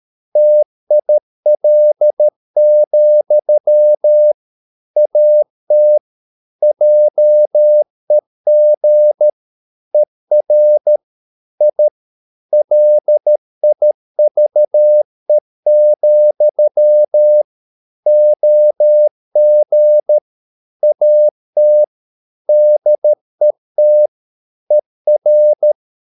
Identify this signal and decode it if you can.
Morse code, heard as a keyed tone, about 13 wpm, 600 Hz. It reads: TIL, AT JEG ER I LIVE, OG AT DET ER
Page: Ferie 13 wpm